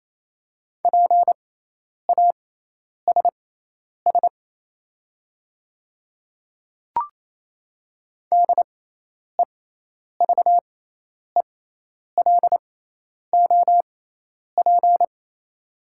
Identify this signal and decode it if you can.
Morse code, heard as a keyed tone, about 27 wpm, spaced as for 10 wpm, 700 Hz. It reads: PASS DEVELOP
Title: 6x Character Spacing - Top 500 Words - 28wpm